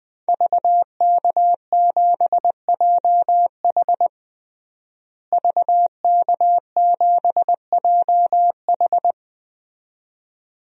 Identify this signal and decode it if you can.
Morse code, heard as a keyed tone, about 20 wpm, 700 Hz. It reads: VK7JH VK7JH
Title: DXCC-100 Call Signs - 5 Characters - Repeated 2x - 20wpm